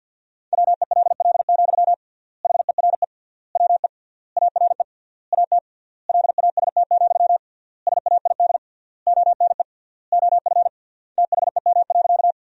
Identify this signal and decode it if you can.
Morse code, heard as a keyed tone, about 50 wpm, 700 Hz. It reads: WELL, HERE WE ARE AT LAST, SAID ONE OF THEM.
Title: 3x Word-Spacing - Sentences from Top 300 Words - 50wpm